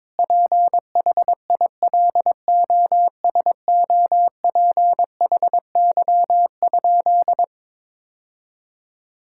Morse code, heard as a keyed tone, about 22 wpm, 700 Hz. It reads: PHILOSOPHY?
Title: Sentences from Top 1000 Words - Review Part 6 - 22wpm